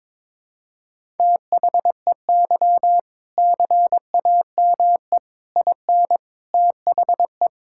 Morse code, heard as a keyed tone, about 22 wpm, 700 Hz. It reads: THEY CAME IN THE
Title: Sentences from Top 1000 Words - Review Part 6 - 22wpm